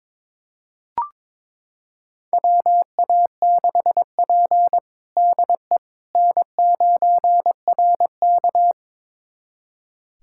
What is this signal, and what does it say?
Morse code, heard as a keyed tone, about 22 wpm, 700 Hz. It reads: WA6P DE N9RK
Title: USA Callsign DE Callsign - 4 Characters - 22wpm